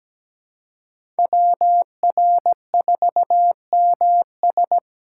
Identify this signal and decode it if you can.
Morse code, heard as a keyed tone, about 17 wpm, 700 Hz. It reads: WR4MS
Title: USA Call Signs - 6x Character Spacing - 17wpm